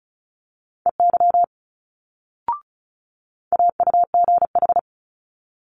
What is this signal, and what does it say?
Morse code, heard as a keyed tone, about 36 wpm, 700 Hz. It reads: EY AUGH
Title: Multi-letter Phonemes Quick Fire 35wpm